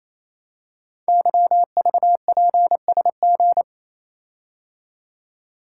Morse code, heard as a keyed tone, about 28 wpm, 700 Hz. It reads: YVPSG
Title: Code Groups - 5 Letters - 28wpm